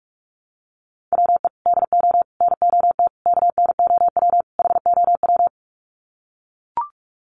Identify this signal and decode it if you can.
Morse code, heard as a keyed tone, about 44 wpm, 700 Hz. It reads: WE DO NOT KNOW HOW
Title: Sets of 5 Words Encore 45wpm